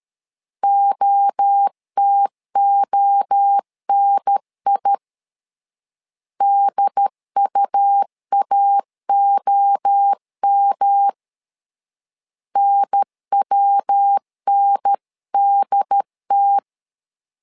Morse code, heard as a keyed tone, about 13 wpm, 800 Hz. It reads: OTONI DUAOM NWNDT
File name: LEZIONI / 3 MWRO / 3 MWRO 12wpm 1730